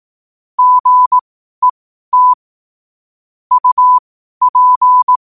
Morse code, heard as a keyed tone, about 18 wpm, spaced as for 12 wpm, 1000 Hz. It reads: GET UP
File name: qotdfiles / 12wpm